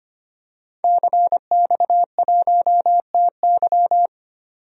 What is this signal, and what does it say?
Morse code, heard as a keyed tone, about 25 wpm, 700 Hz. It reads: CX1TY
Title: DXCC-100 Call Signs - 5 Characters - 25wpm